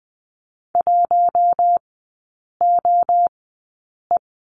Morse code, heard as a keyed tone, about 20 wpm, spaced as for 8 wpm, 700 Hz. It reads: 1OE
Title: 074 Lesson 13e - Letter L - Three Characters 20wpm